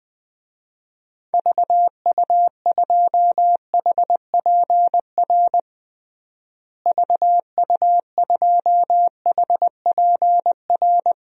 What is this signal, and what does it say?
Morse code, heard as a keyed tone, about 20 wpm, 700 Hz. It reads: VU2HPR VU2HPR
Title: DXCC-100 Call Signs - 6 Characters - Repeated 2x - 20wpm